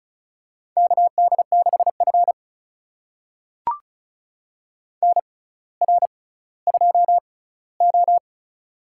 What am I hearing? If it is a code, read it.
Morse code, heard as a keyed tone, about 36 wpm, spaced as for 18 wpm, 700 Hz. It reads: KD6F NR2O
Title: USA Call Signs - 4 Characters - 6x Character Spacing - 35wpm